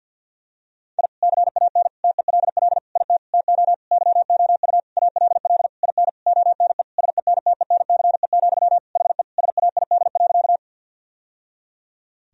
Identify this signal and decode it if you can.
Morse code, heard as a keyed tone, about 50 wpm, 700 Hz. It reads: I CAN TELL IT TO YOU ALL IN ONE SENTENCE, HE SAID.
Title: Sentences from Top 1000 Words - Review Part 1 - 50wpm